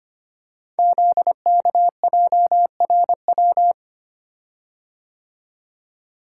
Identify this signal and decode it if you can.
Morse code, heard as a keyed tone, about 25 wpm, 700 Hz. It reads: ZKJRW
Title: Code Groups - 5 Letter:Numbers - 25wpm